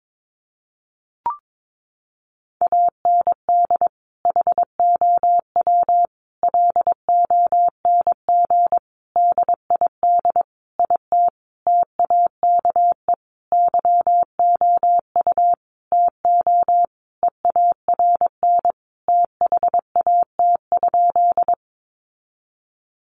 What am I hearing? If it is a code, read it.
Morse code, heard as a keyed tone, about 22 wpm, 700 Hz. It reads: AND HOW LONG DID IT TAKE YOU TO EARN THAT?
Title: Sentences from Top 1500 Words 22wpm